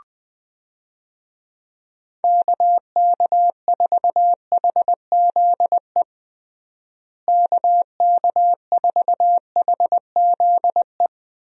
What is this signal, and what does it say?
Morse code, heard as a keyed tone, about 20 wpm, 700 Hz. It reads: KK4HZE KK4HZE
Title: USA Call Signs - 6 Characters - Repeated 2x - 20wpm